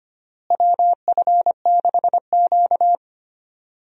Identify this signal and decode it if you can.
Morse code, heard as a keyed tone, about 25 wpm, 700 Hz. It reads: WF6Q
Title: USA Call Signs - 4 Characters - Repeated 3x - 25wpm